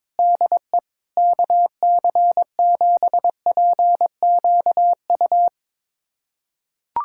Morse code, heard as a keyed tone, about 22 wpm, 700 Hz. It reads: DE KC7PQU
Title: USA Callsign DE Callsign - 6 Characters - 22wpm